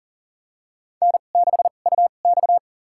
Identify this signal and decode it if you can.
Morse code, heard as a keyed tone, about 40 wpm, spaced as for 29 wpm, 700 Hz. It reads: N6UX
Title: USA Call Signs - 4 Characters - 2x Character Spacing - 40wpm